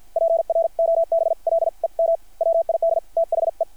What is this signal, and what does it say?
Morse code, heard as a keyed tone, about 50 wpm, spaced as for 43 wpm, 660 Hz. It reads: PROBLEM WID THE
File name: archived / CRSnet